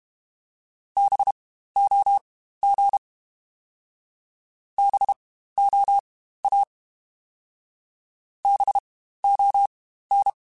Morse code, heard as a keyed tone, about 32 wpm, spaced as for 15 wpm, 800 Hz. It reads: BOG BOA BON
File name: Spaced Alphas / Spaced Alphas mp3s / Bs sp @30